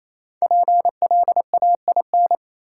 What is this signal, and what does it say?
Morse code, heard as a keyed tone, about 28 wpm, 700 Hz. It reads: PLAIN
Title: Sentences from Top 400 Words - New Word Drill - 28wpm